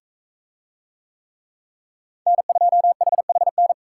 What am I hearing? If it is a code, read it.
Morse code, heard as a keyed tone, about 40 wpm, 700 Hz. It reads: N2HHN